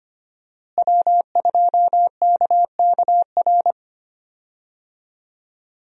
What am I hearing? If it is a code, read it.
Morse code, heard as a keyed tone, about 25 wpm, 700 Hz. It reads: W2KKR
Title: USA Call Signs - 5 Characters - Repeated 2x - 25wpm